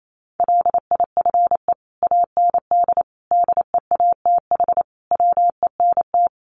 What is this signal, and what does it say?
Morse code, heard as a keyed tone, about 28 wpm, 700 Hz. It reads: LIFE AND DEATH WENT
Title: Sentences from Top 1100 Words 28wpm